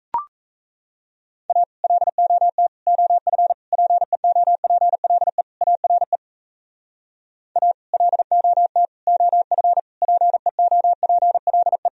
Morse code, heard as a keyed tone, about 35 wpm, 700 Hz. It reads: A LOT OF PEOPLE ARE A LOT OF PEOPLE
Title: Speed-Racing - Sets of 5 Words 28wpm